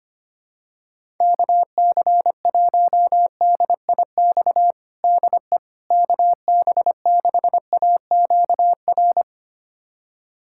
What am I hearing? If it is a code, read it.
Morse code, heard as a keyed tone, about 25 wpm, 700 Hz. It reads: KC1DIX DE KB6AQR
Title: USA Callsign DE Callsign 25wpm